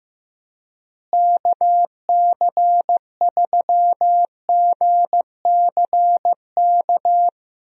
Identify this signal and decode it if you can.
Morse code, heard as a keyed tone, about 15 wpm, 700 Hz. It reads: KC3GCK
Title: USA Call Signs - 6 Characters - Repeated 3x - 15wpm